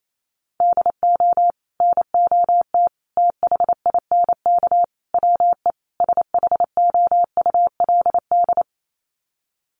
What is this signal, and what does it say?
Morse code, heard as a keyed tone, about 28 wpm, 700 Hz. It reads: DO NOT THINK WE SHOULD
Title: Sets of 5 Words Encore 28wpm